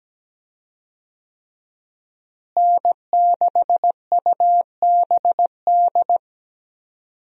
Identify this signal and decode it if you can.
Morse code, heard as a keyed tone, about 17 wpm, 700 Hz. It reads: N6UBD
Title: USA Call Signs - 5 Characters - Repeated 3x - 17wpm